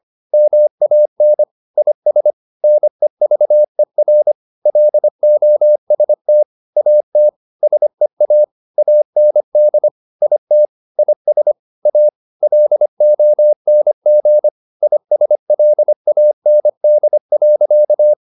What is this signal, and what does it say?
Morse code, heard as a keyed tone, about 25 wpm, 600 Hz. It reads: MAN IS NEVER LOST AT SEA AND IT IS A LONG ISLAND.